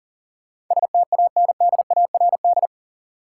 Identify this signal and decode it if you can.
Morse code, heard as a keyed tone, about 40 wpm, 700 Hz. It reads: STANDARD
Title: Sentences from Top 1200 Words - New Word Drill Alt - 40wpm